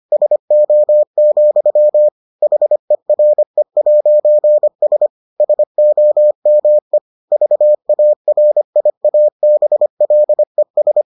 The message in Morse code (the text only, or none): SO, HERE'S SOME VARIABLES